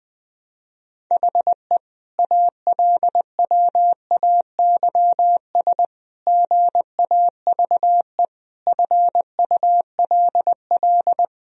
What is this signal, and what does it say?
Morse code, heard as a keyed tone, about 21 wpm, 700 Hz. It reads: HE ALWAYS GAVE FULL
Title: Sentences from Top 1000 Words - Review Part 3 - 20wpm